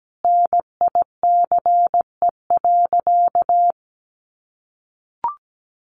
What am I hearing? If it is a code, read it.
Morse code, heard as a keyed tone, about 17 wpm, 700 Hz. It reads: NICE.
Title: Easy Phrases 17wpm